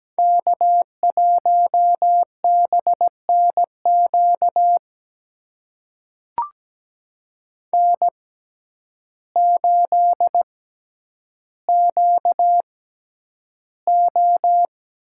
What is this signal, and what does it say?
Morse code, heard as a keyed tone, about 17 wpm, spaced as for 9 wpm, 700 Hz. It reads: K1BNQ N8QO
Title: USA Call Signs - 6x Character Spacing - 17wpm